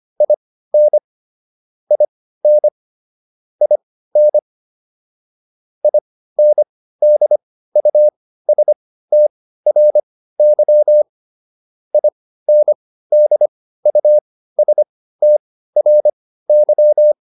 Morse code, heard as a keyed tone, about 25 wpm, spaced as for 15 wpm, 600 Hz. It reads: IN IN IN INDUSTRY INDUSTRY
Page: intermediate-practice-files